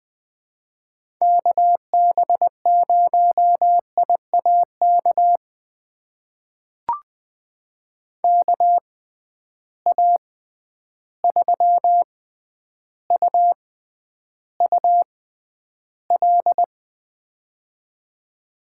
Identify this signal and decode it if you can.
Morse code, heard as a keyed tone, about 20 wpm, spaced as for 10 wpm, 700 Hz. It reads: KB0IAK KA3UUL
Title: USA Call Signs - 6x Character Spacing - 20wpm